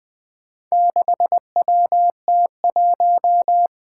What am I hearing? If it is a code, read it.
Morse code, heard as a keyed tone, about 20 wpm, 700 Hz. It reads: 6WT1
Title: Four Challenging Letter-Number Combinations 20wpm